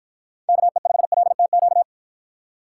Morse code, heard as a keyed tone, about 53 wpm, 700 Hz. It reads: KE5LTQ